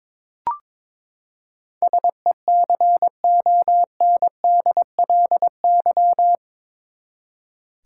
Morse code, heard as a keyed tone, about 22 wpm, 700 Hz. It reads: SECONDLY
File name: practice / 22wpm / Eight Letter Words - 22wpm